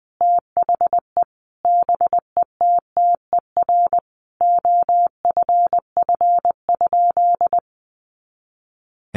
Morse code, heard as a keyed tone, about 20 wpm, 700 Hz. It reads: THE BETTER OFF?
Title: Sentences from Top 500 Words 20wpm